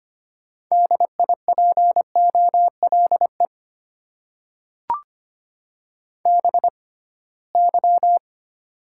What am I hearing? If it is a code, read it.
Morse code, heard as a keyed tone, about 25 wpm, spaced as for 19 wpm, 700 Hz. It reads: DIPOLE BY
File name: Top 90 QSO Elements - 6x Character Spacing - 25wpm